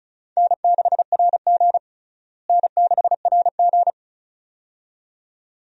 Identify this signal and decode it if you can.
Morse code, heard as a keyed tone, about 36 wpm, 700 Hz. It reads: N6RG N6RG
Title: USA Call Signs - 4 Characters - Repeated 2x - 35wpm